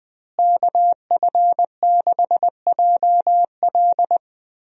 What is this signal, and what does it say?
Morse code, heard as a keyed tone, about 21 wpm, 700 Hz. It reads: KF6JL
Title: USA Call Signs - 5 Characters - Repeated 3x - 20wpm